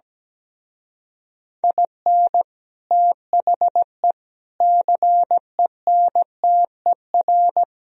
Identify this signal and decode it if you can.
Morse code, heard as a keyed tone, about 17 wpm, 700 Hz. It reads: IN THE CENTER
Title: Speed-Racing - Sets of 5 Words 17wpm